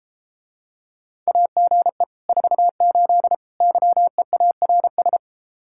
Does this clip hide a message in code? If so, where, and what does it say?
Yes: Morse code from 1.3 to 5.2 s, heard as a keyed tone, about 33 wpm, 700 Hz. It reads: AGE 48 YEARS